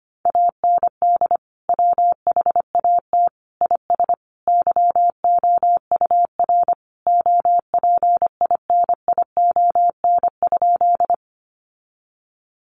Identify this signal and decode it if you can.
Morse code, heard as a keyed tone, about 25 wpm, 700 Hz. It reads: AND WHAT IS YOUR OPINION?
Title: Sentences from Top 1200 Words 25wpm